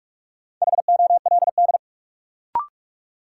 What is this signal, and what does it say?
Morse code, heard as a keyed tone, about 41 wpm, 700 Hz. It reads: HOLD
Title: Sentences from Top 400 Words - New Word Drill - 45wpm